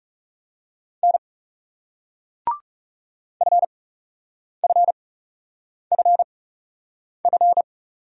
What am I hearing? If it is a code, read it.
Morse code, heard as a keyed tone, about 41 wpm, spaced as for 8 wpm, 700 Hz. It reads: N FFFF